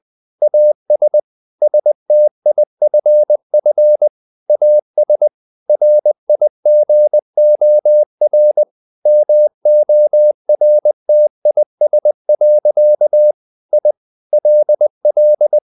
Morse code, heard as a keyed tone, about 20 wpm, 600 Hz. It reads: AS STIFF AS RIGOR MORTIS. I LL
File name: the old man and the sea / the old man and the sea part 1 20wpm